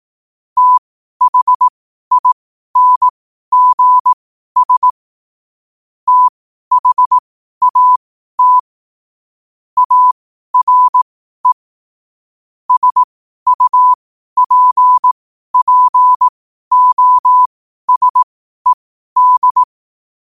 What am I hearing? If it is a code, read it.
Morse code, heard as a keyed tone, about 18 wpm, spaced as for 12 wpm, 1000 Hz. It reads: THINGS THAT ARE SUPPOSED
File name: qotdfiles / 12wpm